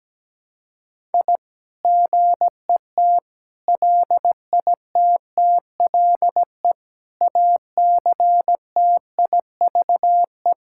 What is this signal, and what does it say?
Morse code, heard as a keyed tone, about 17 wpm, 700 Hz. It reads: I GET LITTLE ACTIVE
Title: Sentences from Top 1000 Words - Review Part 6 - 17wpm